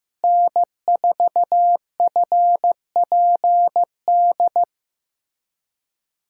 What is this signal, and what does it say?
Morse code, heard as a keyed tone, about 16 wpm, 700 Hz. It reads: N4FPD
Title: USA Callsign DE Callsign - 5 Characters - 15wpm